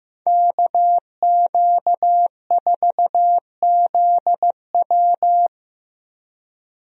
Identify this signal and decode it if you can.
Morse code, heard as a keyed tone, about 15 wpm, 700 Hz. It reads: KQ4ZW
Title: USA Call Signs - 5 Characters - Repeated 2x - 15wpm